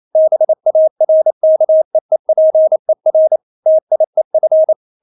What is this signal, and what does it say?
Morse code, heard as a keyed tone, about 28 wpm, 630 Hz. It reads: BARKEEPER TIEF
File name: Witz9 28wpm